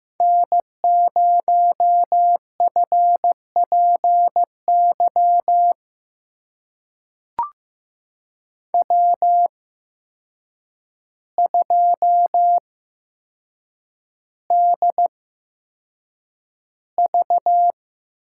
Morse code, heard as a keyed tone, about 16 wpm, spaced as for 7 wpm, 700 Hz. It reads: N0FPY W2DV
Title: USA Call Signs - 5 Characters - 8x Character Spacing - 15wpm